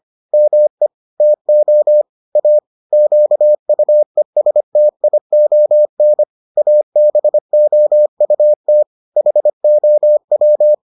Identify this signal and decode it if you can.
Morse code, heard as a keyed tone, about 25 wpm, 600 Hz. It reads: ME TO A QUESTION ABOUT HOW